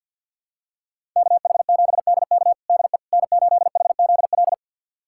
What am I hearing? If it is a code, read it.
Morse code, heard as a keyed tone, about 51 wpm, 700 Hz. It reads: KH7DK DE N8HZL
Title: USA Callsign DE Callsign - 5 Characters - 50wpm